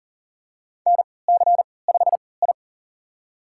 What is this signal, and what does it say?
Morse code, heard as a keyed tone, about 41 wpm, spaced as for 23 wpm, 700 Hz. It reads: NC5I
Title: USA Call Signs - 4 Characters - 3x Character Spacing - 40wpm